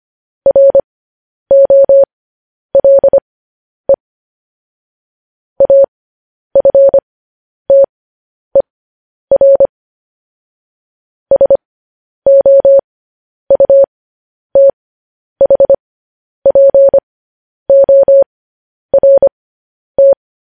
Morse code, heard as a keyed tone, about 25 wpm, spaced as for 10 wpm, 550 Hz. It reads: ROLE AFTER SOUTHPORT
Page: news-headlines-in-morse-code-at-10-wpm